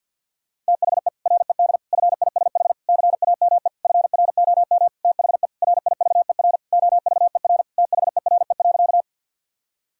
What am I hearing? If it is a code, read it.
Morse code, heard as a keyed tone, about 49 wpm, 700 Hz. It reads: THE RED FISH CAME FROM THE RIVER OVER THERE.
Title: Sentences from Top 1000 Words - Review Part 2 - 50wpm